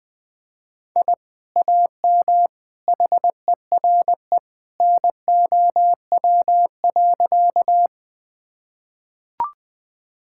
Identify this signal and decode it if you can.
Morse code, heard as a keyed tone, about 20 wpm, 700 Hz. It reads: I AM HERE NOW.
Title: Sentences with 2-Word Phrases - 20wpm